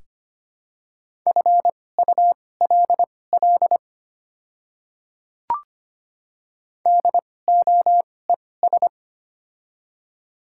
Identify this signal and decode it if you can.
Morse code, heard as a keyed tone, about 25 wpm, spaced as for 18 wpm, 700 Hz. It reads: FULL DOES
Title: 2x Character Spacing - Top 500 Words - 25wpm